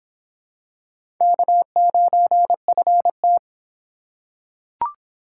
Morse code, heard as a keyed tone, about 25 wpm, 700 Hz. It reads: K9FT